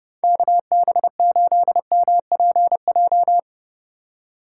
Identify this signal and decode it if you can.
Morse code, heard as a keyed tone, about 30 wpm, 700 Hz. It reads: KB8MPJ